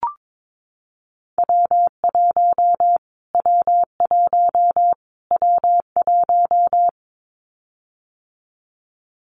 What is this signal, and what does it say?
Morse code, heard as a keyed tone, about 22 wpm, 700 Hz. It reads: W1 W1 W1
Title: Callsign Prefixes for Top 25 DXCC - Repeated 3x - 22wpm